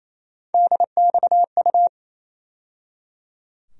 Morse code, heard as a keyed tone, about 28 wpm, 700 Hz. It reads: DXU
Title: USA Call Sign Postfixes - 3 Letters 28wpm